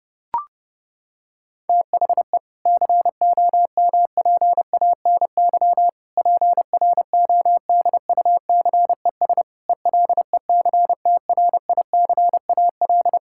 Morse code, heard as a keyed tone, about 32 wpm, 700 Hz. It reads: THE COMPANY PRODUCES ELECTRICAL
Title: Sentences from Top 1000 Words - Review Part 6 - 30wpm